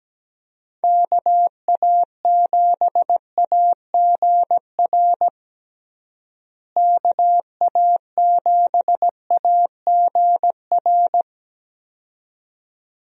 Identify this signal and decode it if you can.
Morse code, heard as a keyed tone, about 17 wpm, 700 Hz. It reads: KA7AGR KA7AGR
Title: USA Call Signs - 6 Characters - Repeated 2x - 17wpm